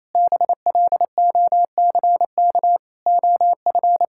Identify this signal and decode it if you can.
Morse code, heard as a keyed tone, about 28 wpm, 700 Hz. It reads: BLOCK OF
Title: Sentences from Top 1600 Words 28wpm